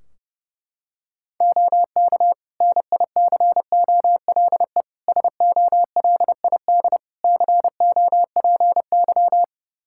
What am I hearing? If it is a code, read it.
Morse code, heard as a keyed tone, about 30 wpm, 700 Hz. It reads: OK NICOLE SOLID COPY
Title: Practice QSO - Exchange 3 - More Info - 30wpm